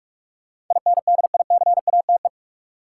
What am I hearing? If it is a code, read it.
Morse code, heard as a keyed tone, about 41 wpm, 700 Hz. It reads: INDICATE
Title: Sentences from Top 1300 Words - New Word Drill - 45wpm